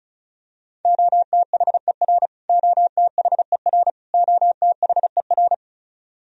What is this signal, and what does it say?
Morse code, heard as a keyed tone, about 36 wpm, 700 Hz. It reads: OTHER OTHER OTHER
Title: Top 100 Words Repeated 35wpm